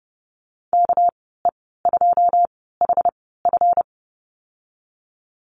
Morse code, heard as a keyed tone, about 32 wpm, spaced as for 17 wpm, 700 Hz. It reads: KE2HF
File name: Lesson 37g - Letter x - Call Signs 30wpm